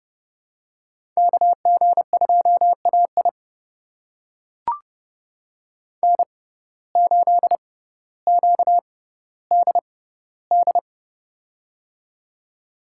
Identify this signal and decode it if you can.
Morse code, heard as a keyed tone, about 30 wpm, spaced as for 15 wpm, 700 Hz. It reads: KG2AI N8QDD
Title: USA Call Signs - 5 Characters - 6x Character Spacing - 30wpm